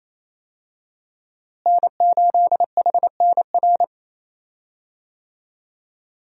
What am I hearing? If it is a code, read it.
Morse code, heard as a keyed tone, about 27 wpm, 700 Hz. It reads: N8HNR
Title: USA Call Signs - 5 Characters - Repeated 3x - 28wpm